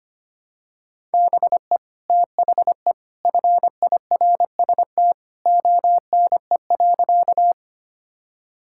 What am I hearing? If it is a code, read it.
Morse code, heard as a keyed tone, about 25 wpm, 700 Hz. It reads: BE THE FIRST ONE.
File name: practice / 25wpm / Sentences with 2-Word Phrases - 25wpm